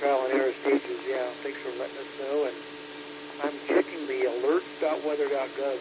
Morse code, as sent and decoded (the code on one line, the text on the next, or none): none